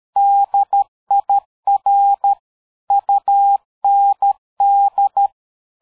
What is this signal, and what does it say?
Morse code, heard as a keyed tone, about 13 wpm, 800 Hz. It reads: DIR UND
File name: dk5ke / PersWorte 60